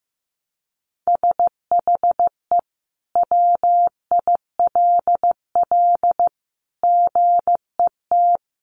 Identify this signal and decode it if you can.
Morse code, heard as a keyed tone, about 15 wpm, 700 Hz. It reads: SHE WILL GET
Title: Sentences from Top 1200 Words 15wpm